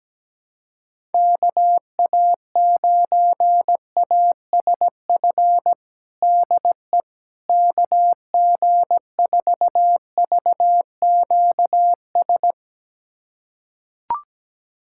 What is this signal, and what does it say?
Morse code, heard as a keyed tone, about 17 wpm, 700 Hz. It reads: KA9ASF DE KG4VQS
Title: USA Callsign DE Callsign 17wpm